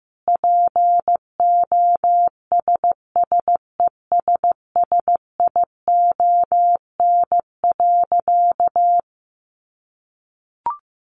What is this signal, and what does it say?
Morse code, heard as a keyed tone, about 15 wpm, 700 Hz. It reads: POSSESSION.
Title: Sentences from Top 1400 Words 15wpm